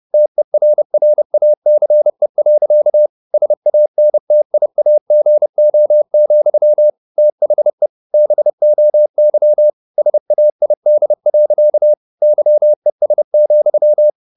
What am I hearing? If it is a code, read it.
Morse code, heard as a keyed tone, about 30 wpm, 600 Hz. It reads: TERRACE. SANTIAGO, THE BOY SAID. YES,
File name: the old man and the sea / the old man and the sea 1 30wpm